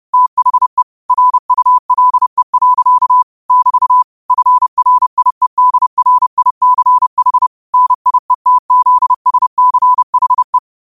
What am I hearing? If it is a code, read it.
Morse code, heard as a keyed tone, about 30 wpm, 1000 Hz. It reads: THE RULE. <BT> FRIEDRICH NIETZSCHE
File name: qotdfiles / 30wpm